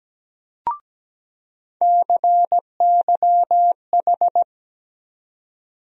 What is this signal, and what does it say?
Morse code, heard as a keyed tone, about 17 wpm, 700 Hz. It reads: CYH
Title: USA Call Sign Postfixes - 3 Letters 17wpm